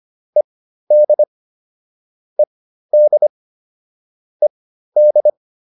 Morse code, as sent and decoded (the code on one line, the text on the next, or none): . -.. / . -.. / . -..
ED ED ED